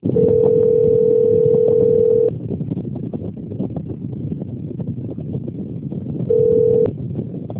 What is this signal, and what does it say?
Telephone call progress tone (North America), heard as a telephone network's tone, about 440, 480 Hz, with noise. Ringback tone.